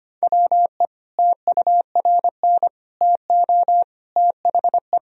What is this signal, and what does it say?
Morse code, heard as a keyed tone, about 25 wpm, 700 Hz. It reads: WE TURN TO THE
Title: Sentences from Top 1000 Words - Review Part 1 - 25wpm